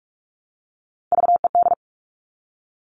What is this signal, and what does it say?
Morse code, heard as a keyed tone, about 44 wpm, 700 Hz. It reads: VED
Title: All Cut Numbers - 3-Digit 45wpm